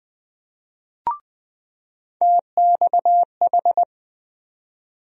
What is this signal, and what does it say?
Morse code, heard as a keyed tone, about 20 wpm, 700 Hz. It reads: TXH
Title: USA Call Sign Postfixes - 3 Letters 20wpm